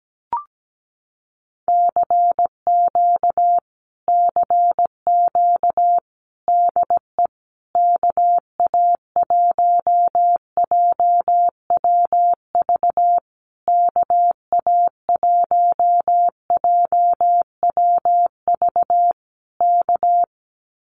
Morse code, heard as a keyed tone, about 17 wpm, 700 Hz. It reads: CQ CQ DE KA1JWV KA1JWV K
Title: Common QSO Phrases Encore 17wpm